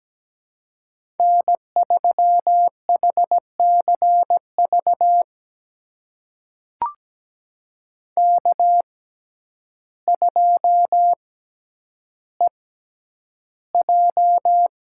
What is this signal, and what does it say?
Morse code, heard as a keyed tone, about 17 wpm, spaced as for 9 wpm, 700 Hz. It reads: N3HCV K2EJ
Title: USA Call Signs - 5 Characters - 6x Character Spacing - 17wpm